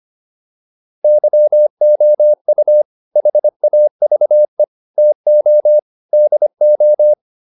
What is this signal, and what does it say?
Morse code, heard as a keyed tone, about 25 wpm, 600 Hz. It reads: YOU HAVE TO DO
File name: WD304 25